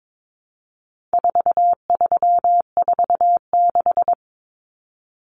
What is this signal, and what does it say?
Morse code, heard as a keyed tone, about 22 wpm, 700 Hz. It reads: 4346
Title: Numbers - 4-Digit 22wpm